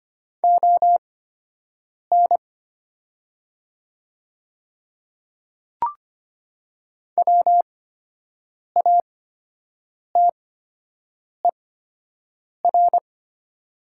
Morse code, heard as a keyed tone, about 25 wpm, spaced as for 7 wpm, 700 Hz. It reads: ON WATER
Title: 8x Character Spacing - Top 100 Words - 25wpm